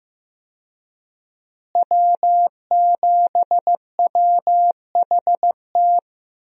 Morse code, heard as a keyed tone, about 15 wpm, 700 Hz. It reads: W7WHT
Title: USA Call Signs - 5 Characters - Repeated 3x - 15wpm